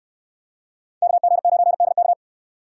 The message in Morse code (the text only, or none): KG9DK